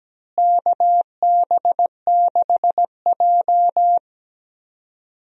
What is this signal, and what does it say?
Morse code, heard as a keyed tone, about 17 wpm, 700 Hz. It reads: KB6J
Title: USA Call Signs - 4 Characters - Repeated 2x - 17wpm